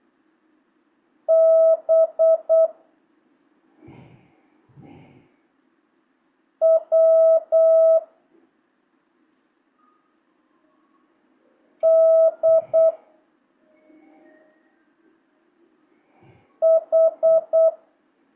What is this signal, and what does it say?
Morse code, heard as a keyed tone, about 8 wpm, 650 Hz. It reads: B W D H
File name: jh8cbh / 2025cw